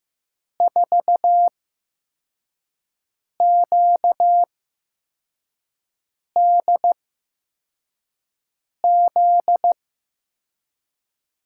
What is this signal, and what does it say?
Morse code, heard as a keyed tone, about 15 wpm, spaced as for 4 wpm, 700 Hz. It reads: 4QDZ
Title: USA Call Signs - 6 Characters - 8x Character Spacing - 15wpm